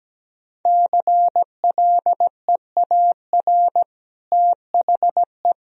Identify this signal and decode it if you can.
Morse code, heard as a keyed tone, about 17 wpm, 700 Hz. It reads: CLEAR THE
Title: Sentences from Top 1000 Words - Review Part 2 - 17wpm